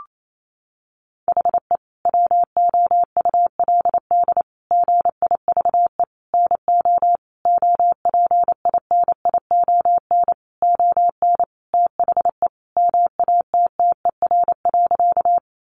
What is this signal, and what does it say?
Morse code, heard as a keyed tone, about 28 wpm, 700 Hz. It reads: HE WOULD GIVE NO OPINION ON THE MATTER.
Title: Sentences from Top 1200 Words 28wpm